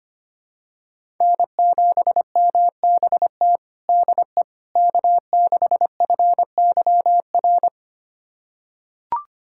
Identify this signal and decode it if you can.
Morse code, heard as a keyed tone, about 25 wpm, 700 Hz. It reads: N7MBT DE K6FYR
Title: USA Callsign DE Callsign - 5 Characters - 25wpm